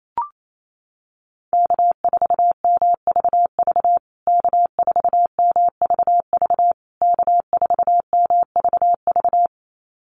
Morse code, heard as a keyed tone, about 28 wpm, 700 Hz. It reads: K4MVV K4MVV K4MVV
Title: USA Call Signs Repeated 3x 28wpm